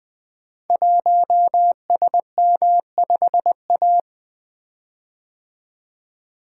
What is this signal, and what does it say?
Morse code, heard as a keyed tone, about 20 wpm, 700 Hz. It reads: 1SM5A
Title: Code Groups - 5 Letter:Numbers - 20wpm